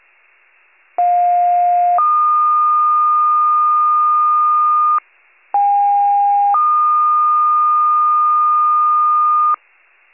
Two-tone page, heard as a one-second tone, then a three-second tone, about 710 then 1220 Hz, about 810 then 1220 Hz. It is heard from 1.0 to 9.6 s.
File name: wcalert